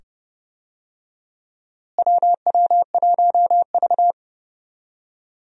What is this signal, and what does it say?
Morse code, heard as a keyed tone, about 30 wpm, 700 Hz. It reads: WW1V
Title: USA Call Signs - 4 Characters - Repeated 2x - 30wpm